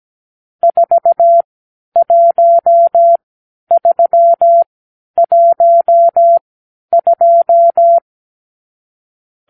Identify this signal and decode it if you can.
Morse code, heard as a keyed tone, about 17 wpm, spaced as for 10 wpm, 680 Hz. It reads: 41312
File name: arrl / Review (1-5)